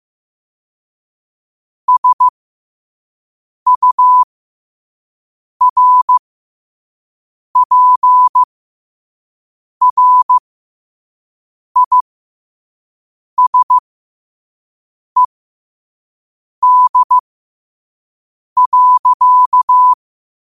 Morse code, heard as a keyed tone, about 15 wpm, spaced as for 5 wpm, 1000 Hz. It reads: SURPRISED.
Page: New quotes every day in morse code at 5 Words per minute.